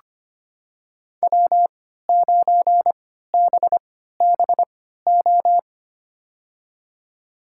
Morse code, heard as a keyed tone, about 25 wpm, spaced as for 14 wpm, 700 Hz. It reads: W9BBO
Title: USA Call Signs - 5 Characters - 3x Character Spacing - 25wpm